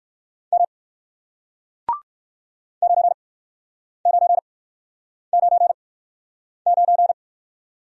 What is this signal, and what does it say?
Morse code, heard as a keyed tone, about 62 wpm, spaced as for 9 wpm, 700 Hz. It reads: M 9999